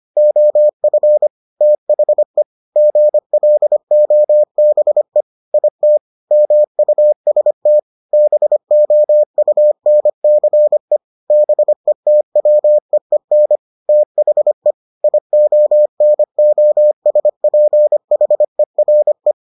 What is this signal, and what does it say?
Morse code, heard as a keyed tone, about 25 wpm, 600 Hz. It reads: OF THE GLOBE IT MUST BOUNCE BETWEEN THE IONOSPHERE